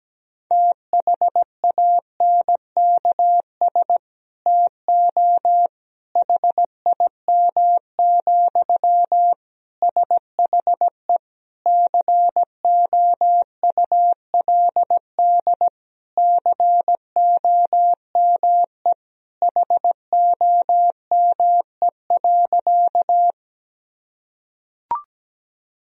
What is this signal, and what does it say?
Morse code, heard as a keyed tone, about 17 wpm, 700 Hz. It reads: THANKS TO HIM, SHE COULD COME HOME.
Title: Sentences from Top 1000 Words - Review Part 3 - 17wpm